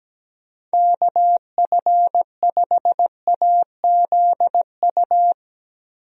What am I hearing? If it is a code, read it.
Morse code, heard as a keyed tone, about 17 wpm, 700 Hz. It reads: KF5AZU
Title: USA Call Signs - 6 Characters - Repeated 2x - 17wpm